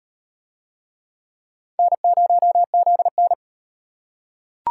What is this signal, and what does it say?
Morse code, heard as a keyed tone, about 38 wpm, 700 Hz. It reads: N0ZN